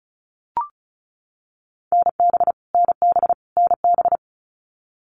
Morse code, heard as a keyed tone, about 34 wpm, 700 Hz. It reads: NB NB NB
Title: Canadian Provinces and Territories - Repeated 3x - 35wpm